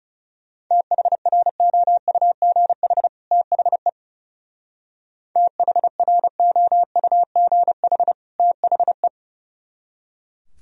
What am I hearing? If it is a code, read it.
Morse code, heard as a keyed tone, about 37 wpm, 700 Hz. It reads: THROUGH THE THROUGH THE
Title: Speed-Racing - Sets of 2 Words 30wpm